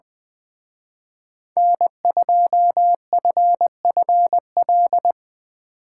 Morse code, heard as a keyed tone, about 21 wpm, 700 Hz. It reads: N2FFL
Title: Speed Racing - USA 1x3 Call Signs 20wpm